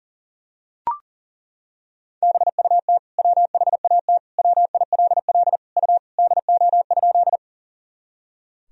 Morse code, heard as a keyed tone, about 40 wpm, 700 Hz. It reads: BUT WHAT WILL U DO?
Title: Sentences from Top 100 Words and Abbreviations 40wpm